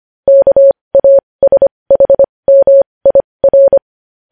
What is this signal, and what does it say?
Morse code, heard as a keyed tone, about 25 wpm, spaced as for 20 wpm, 550 Hz. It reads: KASHMIR
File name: podcast / News headlines-1x-25x20